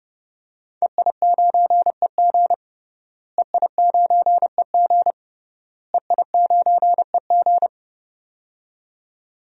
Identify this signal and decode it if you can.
Morse code, heard as a keyed tone, about 30 wpm, 700 Hz. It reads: EI9EG EI9EG EI9EG
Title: DXCC-100 Call Signs - 5 Characters - Repeated 3x - 30wpm